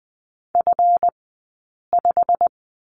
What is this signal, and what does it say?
Morse code, heard as a keyed tone, about 20 wpm, spaced as for 8 wpm, 700 Hz. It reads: F5
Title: 137 Lesson 23d - Letter f - Two Characters 20wpm